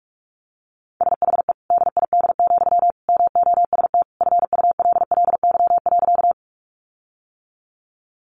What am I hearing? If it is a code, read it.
Morse code, heard as a keyed tone, about 44 wpm, 700 Hz. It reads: SHE DID, MOST FULLY.
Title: Sentences from Top 1300 Words 45wpm